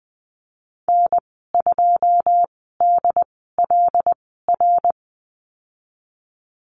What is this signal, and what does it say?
Morse code, heard as a keyed tone, about 20 wpm, spaced as for 15 wpm, 700 Hz. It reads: N2DLR
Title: 082 Lesson 14g - Number 2 - Call Signs 20wpm